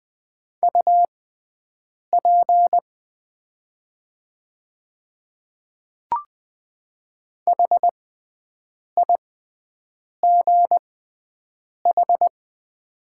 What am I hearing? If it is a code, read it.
Morse code, heard as a keyed tone, about 20 wpm, spaced as for 7 wpm, 690 Hz. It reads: UP HIGH
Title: 6x Character Spacing - Top 200 Words - 20wpm